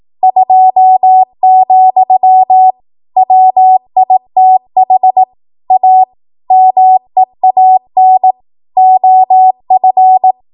Morse code, heard as a keyed tone, about 18 wpm, 750 Hz. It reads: 2, WITH A MEAN OF
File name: Morse / 18WPM